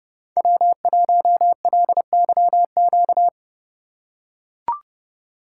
Morse code, heard as a keyed tone, about 31 wpm, 700 Hz. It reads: W1LYQ
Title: USA Call Signs - 4x Character Spacing - 30wpm